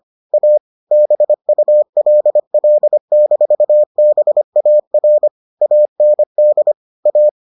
Morse code, heard as a keyed tone, about 25 wpm, 600 Hz. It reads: A BULL-BAR AND A